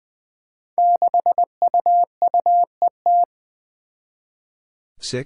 Morse code, heard as a keyed tone, about 21 wpm, 700 Hz. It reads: 6UUET
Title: Five Challenging Letter-Number Combinations 20wpm